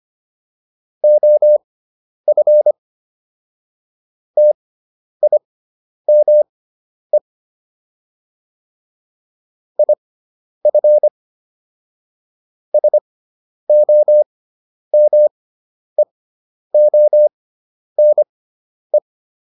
Morse code, heard as a keyed tone, about 25 wpm, spaced as for 10 wpm, 600 Hz. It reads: OF TIME IF SOMEONE